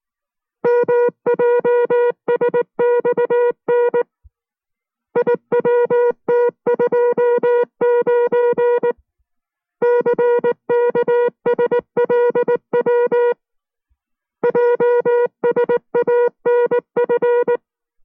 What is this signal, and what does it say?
Morse code, heard as a keyed tone, about 19 wpm, 480 Hz. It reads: MJSXN IWT29 CKSLW JSANF